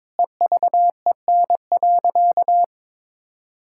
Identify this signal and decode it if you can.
Morse code, heard as a keyed tone, about 22 wpm, 700 Hz. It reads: EVEN.
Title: Sentences from Top 1600 Words 22wpm